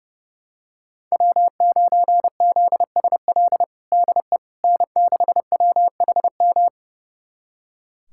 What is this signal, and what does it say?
Morse code, heard as a keyed tone, about 32 wpm, 700 Hz. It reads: W9ZSL DE N6WHM
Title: USA Callsign DE Callsign - 5 Characters - 30wpm